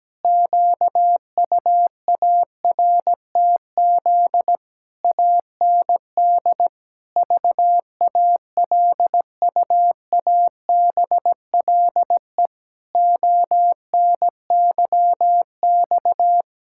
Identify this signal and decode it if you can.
Morse code, heard as a keyed tone, about 17 wpm, 700 Hz. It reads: QUARTZ AND VALUABLE ONYX
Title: Pangram Challenge v2 - 17wpm